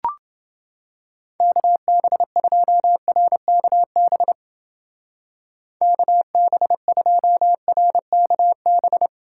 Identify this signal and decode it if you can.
Morse code, heard as a keyed tone, about 28 wpm, 700 Hz. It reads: KB2RKB KB2RKB